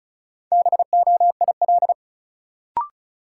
Morse code, heard as a keyed tone, about 36 wpm, 700 Hz. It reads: BOIL
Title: Sets of 5 Words Encore - New Word Drill - 35wpm